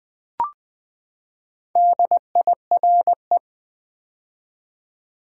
Morse code, heard as a keyed tone, about 20 wpm, 700 Hz. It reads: DIRE
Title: Build A Word - 4 Letters At A Time - 8-Letter Words - 20wpm